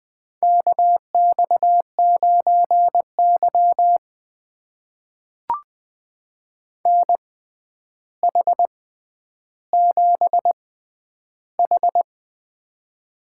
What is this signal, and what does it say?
Morse code, heard as a keyed tone, about 20 wpm, spaced as for 10 wpm, 700 Hz. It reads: KX9Y NH7H
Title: USA Call Signs - 4 Characters - 6x Character Spacing - 20wpm